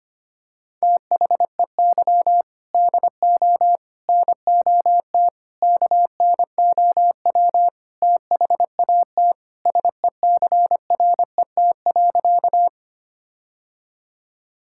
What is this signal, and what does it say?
Morse code, heard as a keyed tone, about 25 wpm, 700 Hz. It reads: THEY DO NOT KNOW THAT SECRET.
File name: practice / 25wpm / Sentences with 4-Word Phrases - 25wpm